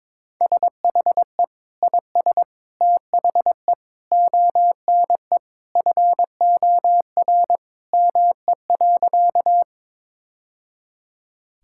Morse code, heard as a keyed tone, about 22 wpm, 700 Hz. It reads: SHE IS THE ONE FOR ME.
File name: practice / 22wpm / Sentences with 2-Word Phrases - 22wpm